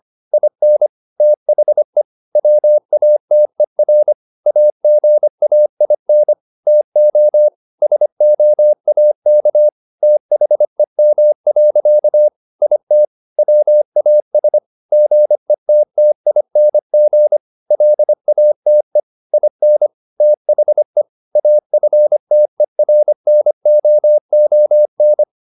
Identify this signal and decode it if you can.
Morse code, heard as a keyed tone, about 25 wpm, 600 Hz. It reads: IN THE WATER AGAIN TO SOAK THEM. IT WAS GETTING LATE IN THE AFTERNOON